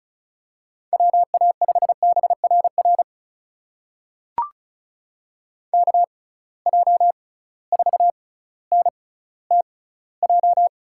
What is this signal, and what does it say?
Morse code, heard as a keyed tone, about 36 wpm, spaced as for 18 wpm, 700 Hz. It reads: WA5BRR KJ4NTJ
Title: USA Call Signs - 6x Character Spacing - 35wpm